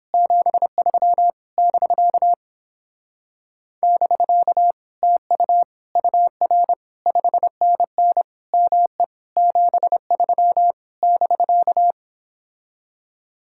Morse code, heard as a keyed tone, about 31 wpm, 700 Hz. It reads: 73 <BK> <BK> TU UR 5NN ME 73 <BK>